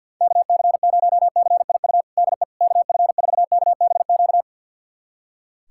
Morse code, heard as a keyed tone, about 52 wpm, 700 Hz. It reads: KC0CIU DE KF4KBQ
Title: USA Callsign DE Callsign - 6 Characters - 50wpm